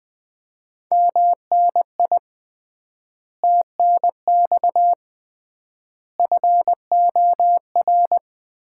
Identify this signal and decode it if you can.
Morse code, heard as a keyed tone, about 20 wpm, 700 Hz. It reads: MNI TNX FOR
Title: Common QSO Phrases - 3x Word-Spacing - 20wpm